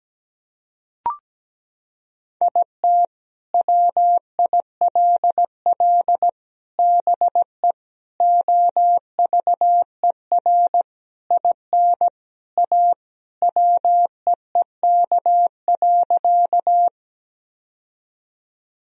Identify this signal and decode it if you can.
Morse code, heard as a keyed tone, about 17 wpm, 700 Hz. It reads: IT WILL BE OVER IN A WEEK.
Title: Sentences from Top 1000 Words - Review Part 2 - 17wpm